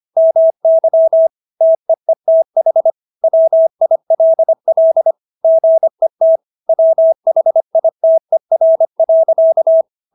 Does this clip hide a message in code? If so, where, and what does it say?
Yes: Morse code from 0.2 to 9.8 s, heard as a keyed tone, about 25 wpm, 650 Hz. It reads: MY TEETH WILL GET WHITER.